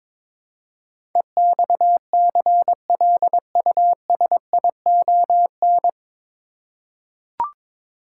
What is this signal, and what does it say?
Morse code, heard as a keyed tone, about 22 wpm, 700 Hz. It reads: EXCLUSION
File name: 9 Letter Words - First 3 Letters Given - 22wpm